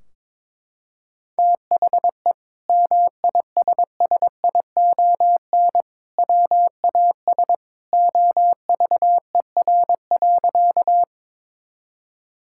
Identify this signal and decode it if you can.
Morse code, heard as a keyed tone, about 22 wpm, 700 Hz. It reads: THE MISSION WAS OVER.
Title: Sentences from Top 1600 Words 22wpm